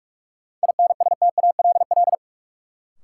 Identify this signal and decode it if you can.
Morse code, heard as a keyed tone, about 46 wpm, 700 Hz. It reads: INSTALL
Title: Sentences from Top 1600 Words - New Word Drill Alt - 45wpm